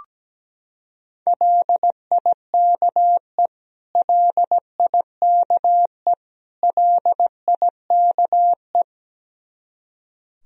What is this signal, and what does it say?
Morse code, heard as a keyed tone, about 17 wpm, 700 Hz. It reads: LIKE LIKE LIKE
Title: Top 100 Words Repeated 17wpm